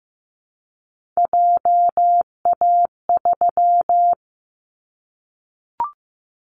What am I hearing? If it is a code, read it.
Morse code, heard as a keyed tone, about 15 wpm, 700 Hz. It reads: JA3
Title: Callsign Prefixes for Top 50 DXCC - 15wpm